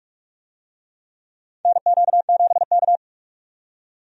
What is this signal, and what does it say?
Morse code, heard as a keyed tone, about 44 wpm, 700 Hz. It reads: NQ7K
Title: USA Call Signs - 4 Characters - 45wpm